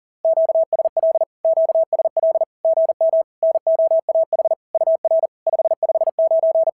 Morse code, heard as a keyed tone, about 39 wpm, 660 Hz. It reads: QSL QSL GM NOAH UR 559
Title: SOTA QSO Practice 40wpm